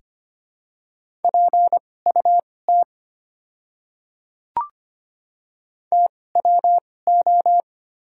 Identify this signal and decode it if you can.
Morse code, heard as a keyed tone, about 25 wpm, spaced as for 18 wpm, 700 Hz. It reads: PUT TWO
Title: 2x Character Spacing - Top 300 Words - 25wpm